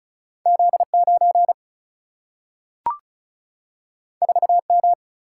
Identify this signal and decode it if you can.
Morse code, heard as a keyed tone, about 36 wpm, 700 Hz. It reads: Z9 4M
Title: Single-Letter Single-Number 35wpm